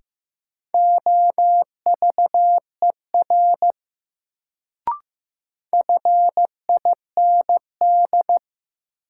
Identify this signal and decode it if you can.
Morse code, heard as a keyed tone, about 15 wpm, 700 Hz. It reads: OVER FIND
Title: Sets of 2 Words - New Word Drill - 15wpm